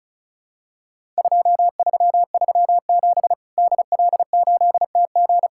Morse code, heard as a keyed tone, about 34 wpm, 700 Hz. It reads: 2337 DL8TG
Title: Worked All Europe DX Contest - QTC Exchange - 35wpm